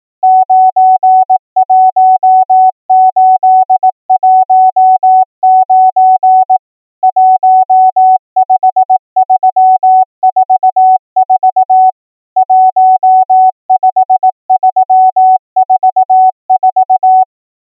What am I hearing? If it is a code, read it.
Morse code, heard as a keyed tone, about 18 wpm, 750 Hz. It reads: 91819 15344 15344
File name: numbers / V2-morse